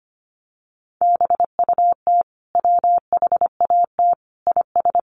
Morse code, heard as a keyed tone, about 25 wpm, 700 Hz. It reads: BUT WHAT IS
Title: Sentences from Top 300 Words 25wpm